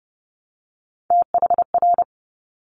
Morse code, heard as a keyed tone, about 31 wpm, 690 Hz. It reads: THR
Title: Top Three Letter Combinations in Words 30wpm